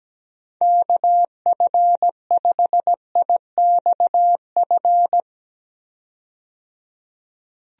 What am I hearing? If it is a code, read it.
Morse code, heard as a keyed tone, about 17 wpm, 700 Hz. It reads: KF5IXF
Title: USA Call Signs - 6 Characters - 17wpm